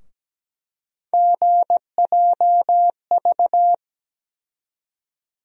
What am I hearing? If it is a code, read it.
Morse code, heard as a keyed tone, about 17 wpm, 700 Hz. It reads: GJV
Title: USA Call Sign Postfixes - 3 Letters 17wpm